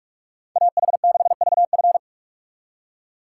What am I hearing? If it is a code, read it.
Morse code, heard as a keyed tone, about 44 wpm, 700 Hz. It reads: AH6VF